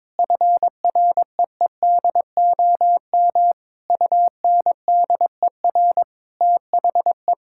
Morse code, heard as a keyed tone, about 22 wpm, 700 Hz. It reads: FREEDOM UNDER THE
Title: Sentences from Top 1000 Words - Review Part 6 - 22wpm